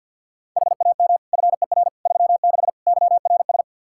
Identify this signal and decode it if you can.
Morse code, heard as a keyed tone, about 52 wpm, 700 Hz. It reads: HAM FER 36 YRS
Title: Practice QSO - Exchange 3 - More Info - 50wpm